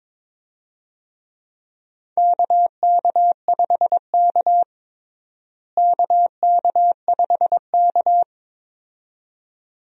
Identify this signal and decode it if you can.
Morse code, heard as a keyed tone, about 22 wpm, 700 Hz. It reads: KK5K KK5K
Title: USA Call Signs - 4 Characters - Repeated 2x - 22wpm